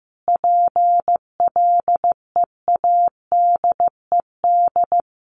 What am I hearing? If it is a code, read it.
Morse code, heard as a keyed tone, about 15 wpm, 700 Hz. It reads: PLEADED
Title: A Sound Of Thunder 15wpm